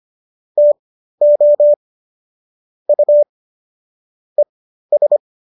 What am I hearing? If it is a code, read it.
Morse code, heard as a keyed tone, about 25 wpm, spaced as for 13 wpm, 590 Hz. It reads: TO U ES
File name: qso214 13